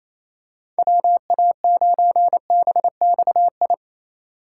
Morse code, heard as a keyed tone, about 28 wpm, 700 Hz. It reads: WA9BXI
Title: USA Call Signs - 6 Characters - 2x Character Spacing - 28wpm